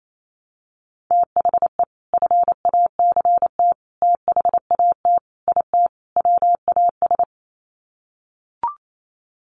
Morse code, heard as a keyed tone, about 28 wpm, 700 Hz. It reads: THE FACT THAT IT WAS
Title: Sets of 5 Words Encore2 28wpm